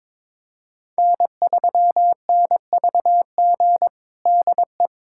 Morse code, heard as a keyed tone, about 22 wpm, 700 Hz. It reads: N3NVG DE
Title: USA Callsign DE Callsign - 5 Characters - 22wpm